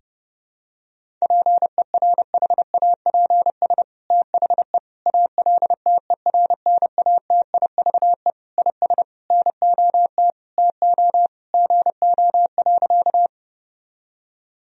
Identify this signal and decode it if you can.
Morse code, heard as a keyed tone, about 32 wpm, 700 Hz. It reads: PERHAPS THE ALTERNATIVE IS NOT TO GO.
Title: Sentences from Top 1000 Words - Review Part 4 - 30wpm